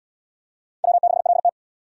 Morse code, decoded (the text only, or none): W6LT